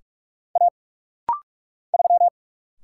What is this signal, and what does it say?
Morse code, heard as a keyed tone, about 45 wpm, 700 Hz. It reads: A 3
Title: Single Letter-Number Alt - Rapid-Fire 45wpm